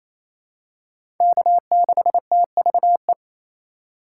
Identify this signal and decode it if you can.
Morse code, heard as a keyed tone, about 28 wpm, 700 Hz. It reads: K6TVE
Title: USA Call Signs - 4x Character Spacing - 28wpm